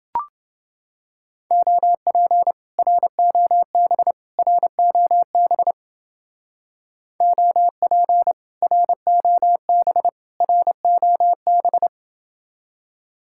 Morse code, heard as a keyed tone, about 29 wpm, 700 Hz. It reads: OP ROB ROB OP ROB ROB